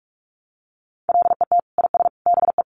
Morse code, heard as a keyed tone, about 44 wpm, 700 Hz. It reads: LET IS BE
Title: Sentences from Top 1200 Words 45wpm